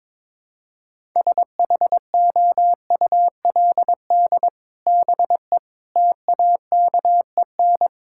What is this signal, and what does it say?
Morse code, heard as a keyed tone, about 22 wpm, 700 Hz. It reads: SHOULD BE TAKEN
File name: practice / 22wpm / Sets of 8 Words - 22wpm